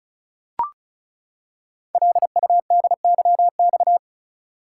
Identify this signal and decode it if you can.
Morse code, heard as a keyed tone, about 36 wpm, 690 Hz. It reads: LUDYX
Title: Code Groups - 5 Letters - 35wpm